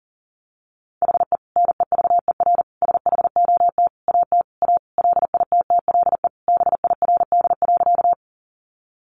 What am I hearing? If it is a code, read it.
Morse code, heard as a keyed tone, about 40 wpm, 700 Hz. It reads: HE NEVER SHOT AT A LITTLE BIRD.
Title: Sentences from Top 1100 Words 40wpm